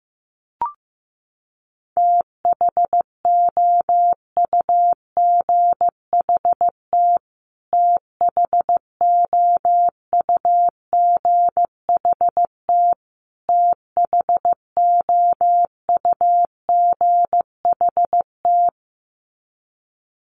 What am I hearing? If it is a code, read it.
Morse code, heard as a keyed tone, about 15 wpm, 700 Hz. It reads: THOUGHT THOUGHT THOUGHT
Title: Top 300 Words Repeated 15wpm